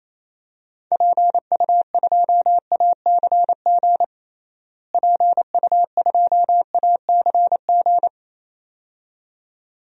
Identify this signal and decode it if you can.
Morse code, heard as a keyed tone, about 28 wpm, 700 Hz. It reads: PU2ACG PU2ACG
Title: DXCC-100 Call Signs - 6 Characters - Repeated 2x - 28wpm